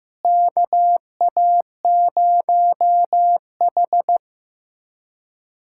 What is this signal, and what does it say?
Morse code, heard as a keyed tone, about 15 wpm, 700 Hz. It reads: KA0H
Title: USA Callsign DE Callsign - 4 Characters - 15wpm